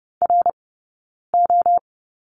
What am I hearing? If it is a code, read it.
Morse code, heard as a keyed tone, about 30 wpm, 700 Hz. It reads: R O
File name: 092 Lesson 16d - Letter c - Two Characters 30wpm